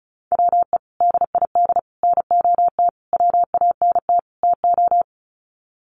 Morse code, heard as a keyed tone, about 34 wpm, 700 Hz. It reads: WE DID NOT WANT TO
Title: Sets of 5 Words Encore 35wpm